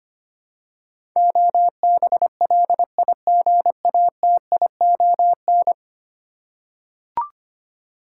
Morse code, heard as a keyed tone, about 25 wpm, 700 Hz. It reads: OBLIGATION
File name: Ten Letter Words - 25wpm